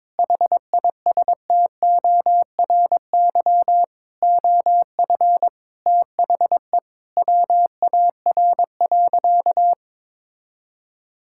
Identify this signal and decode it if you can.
Morse code, heard as a keyed tone, about 22 wpm, 700 Hz. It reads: HISTORY OF THE WAR.
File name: practice / 22wpm / Sentences with 4-Word Phrases - 22wpm